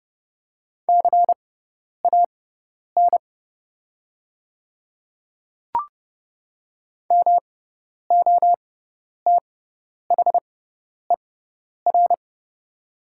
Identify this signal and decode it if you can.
Morse code, heard as a keyed tone, about 31 wpm, spaced as for 10 wpm, 700 Hz. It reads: CAN MOTHER
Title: 6x Character Spacing - Top 200 Words - 30wpm